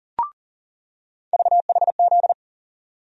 Morse code, heard as a keyed tone, about 40 wpm, 700 Hz. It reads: VHZ
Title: USA Call Sign Postfixes - 3 Letters 40wpm